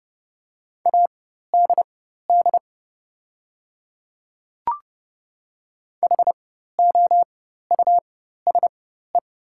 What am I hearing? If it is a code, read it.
Morse code, heard as a keyed tone, about 30 wpm, spaced as for 14 wpm, 700 Hz. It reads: ADD HOUSE
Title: 4x Character Spacing - Top 200 Words - 30wpm